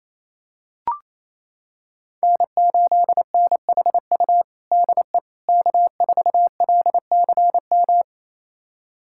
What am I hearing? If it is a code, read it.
Morse code, heard as a keyed tone, about 28 wpm, 700 Hz. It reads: N8NHU DE K4LCM
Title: USA Callsign DE Callsign - 5 Characters - 28wpm